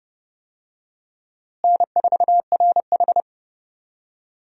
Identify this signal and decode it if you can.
Morse code, heard as a keyed tone, about 30 wpm, 700 Hz. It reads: N4RH
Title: USA Call Signs - 4 Characters - Repeated 2x - 30wpm